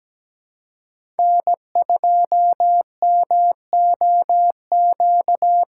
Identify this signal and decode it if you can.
Morse code, heard as a keyed tone, about 17 wpm, 700 Hz. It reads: N2MOQ
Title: USA Call Signs - 5 Characters - 2x Character Spacing - 17wpm